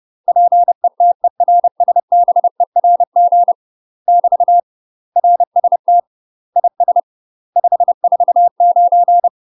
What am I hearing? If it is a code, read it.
Morse code, heard as a keyed tone, about 30 wpm, 700 Hz. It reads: PETERSBERG <BT> RST IS 549
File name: UK MASTER QSOs TRAINING/30WPM